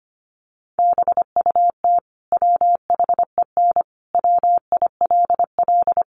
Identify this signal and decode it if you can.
Morse code, heard as a keyed tone, about 25 wpm, 700 Hz. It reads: BUT WHEN WILL
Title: Sentences from Top 100 Words 25wpm